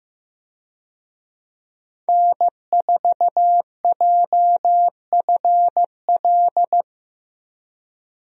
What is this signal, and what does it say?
Morse code, heard as a keyed tone, about 15 wpm, 700 Hz. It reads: N4JFL
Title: USA Call Signs - 5 Characters - 15wpm